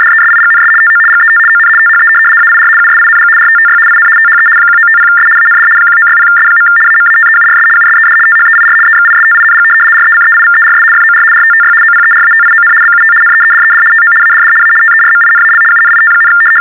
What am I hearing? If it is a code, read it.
RTTY, about 75 baud. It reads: DESIGNS TO GET ON THE AIR USING RTTY IN THE 1950S AND 1960S. AMATEURS USED THEIR EXISTING RECEIVERS FOR RTTY OPERATION BUT NEEDED TO ADD A TERMINAL UNIT,
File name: teletype-RTTY-75